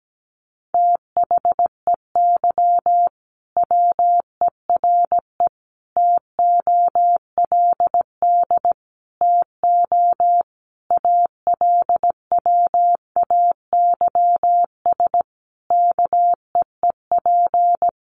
Morse code, heard as a keyed tone, about 17 wpm, 700 Hz. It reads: THEY WERE TOLD TO ALWAYS KEEP
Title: Sentences from Top 1300 Words 17wpm